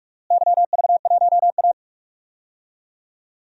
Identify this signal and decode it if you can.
Morse code, heard as a keyed tone, about 47 wpm, 700 Hz. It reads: YV1A
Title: DXCC-100 Call Signs - 4 Characters - Repeated 2x - 45wpm